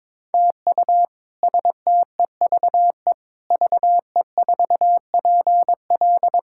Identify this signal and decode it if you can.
Morse code, heard as a keyed tone, about 22 wpm, 700 Hz. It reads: TU STEVE VE4PL
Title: K1USN SST QSO Practice - 22wpm